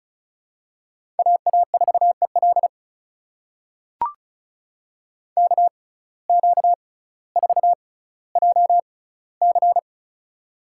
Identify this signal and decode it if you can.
Morse code, heard as a keyed tone, about 36 wpm, spaced as for 18 wpm, 700 Hz. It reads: AA4EL KQ4JC
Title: USA Call Signs - 5 Characters - 6x Character Spacing - 35wpm